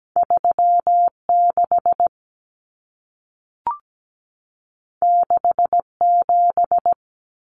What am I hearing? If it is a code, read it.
Morse code, heard as a keyed tone, about 17 wpm, 700 Hz. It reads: 36 67
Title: Numbers - 2-Digit 17wpm